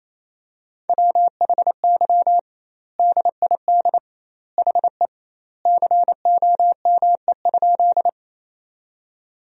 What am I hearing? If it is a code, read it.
Morse code, heard as a keyed tone, about 27 wpm, 700 Hz. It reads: WHY DID HE COME?
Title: 2x Word-Spacing - Sentences from Top 200 Words - 28wpm